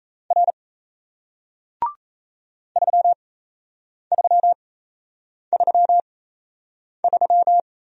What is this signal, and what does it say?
Morse code, heard as a keyed tone, about 38 wpm, spaced as for 8 wpm, 700 Hz. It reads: R 3333